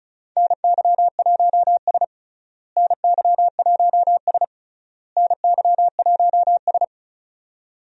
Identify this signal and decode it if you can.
Morse code, heard as a keyed tone, about 36 wpm, 700 Hz. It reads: NY1S NY1S NY1S
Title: USA Call Signs - 4 Characters - Repeated 3x - 35wpm